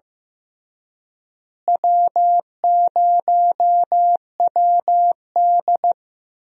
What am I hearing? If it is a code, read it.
Morse code, heard as a keyed tone, about 15 wpm, 700 Hz. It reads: W0WD
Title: USA Call Signs - 4 Characters - Repeated 2x - 15wpm